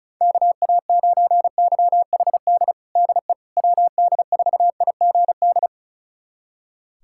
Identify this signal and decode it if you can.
Morse code, heard as a keyed tone, about 36 wpm, 700 Hz. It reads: KA9YHD DE WD4IGD
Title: USA Callsign DE Callsign 35wpm